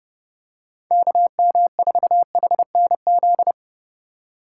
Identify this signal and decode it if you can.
Morse code, heard as a keyed tone, about 30 wpm, 700 Hz. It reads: KM4HNZ